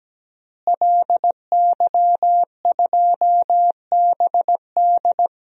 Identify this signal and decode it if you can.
Morse code, heard as a keyed tone, about 17 wpm, 700 Hz. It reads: LY2BD
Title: DXCC-100 Call Signs - 5 Characters - Repeated 3x - 17wpm